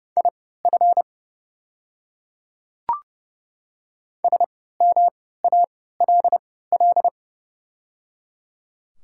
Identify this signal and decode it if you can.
Morse code, heard as a keyed tone, about 30 wpm, spaced as for 17 wpm, 690 Hz. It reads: IF SMALL
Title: 3x Character Spacing - Top 200 Words - 30wpm